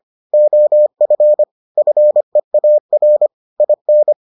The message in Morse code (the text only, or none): OF FEAR IN